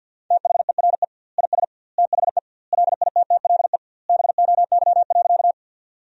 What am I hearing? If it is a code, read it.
Morse code, heard as a keyed tone, about 50 wpm, 700 Hz. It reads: THERE IS THE LITTLE BOY.
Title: 2x Word-Spacing - Sentences from Top 200 Words - 50wpm